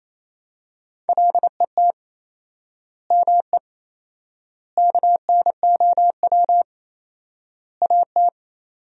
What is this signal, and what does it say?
Morse code, heard as a keyed tone, about 28 wpm, 700 Hz. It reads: LET ME KNOW AT
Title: 4x Word-Spacing - Sentences from Top 300 Words - 28wpm